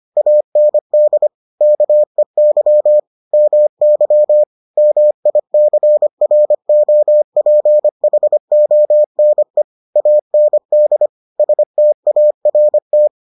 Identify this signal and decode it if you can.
Morse code, heard as a keyed tone, about 25 wpm, 600 Hz. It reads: AND KEY MY MICROPHONE AND START